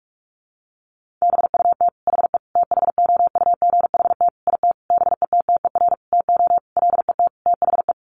Morse code, heard as a keyed tone, about 44 wpm, 700 Hz. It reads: BUT HE THOUGHT IT BETTER TO LET THE
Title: Sentences from Top 1400 Words 45wpm